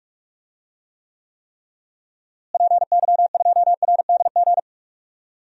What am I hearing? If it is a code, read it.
Morse code, heard as a keyed tone, about 44 wpm, 700 Hz. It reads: PY2RDG
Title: DXCC-100 Call Signs - 6 Characters - Repeated 3x - 45wpm